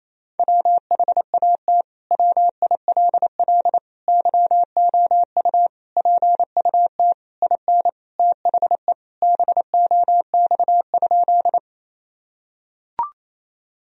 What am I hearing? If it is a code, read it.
Morse code, heard as a keyed tone, about 28 wpm, 700 Hz. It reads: WHAT WILL YOU PUT IN THE BOX?
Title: Sentences from Top 1000 Words - Review Part 2 - 28wpm